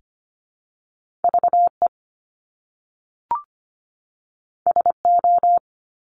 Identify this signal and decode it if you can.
Morse code, heard as a keyed tone, about 25 wpm, 700 Hz. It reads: VE SO
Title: Top Two Letter Combinations in Words 25wpm